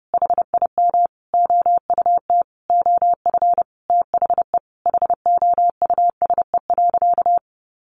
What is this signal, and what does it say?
Morse code, heard as a keyed tone, about 30 wpm, 700 Hz. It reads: HIM OUT OF THE HOUSE.
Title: Sentences from Top 1300 Words 30wpm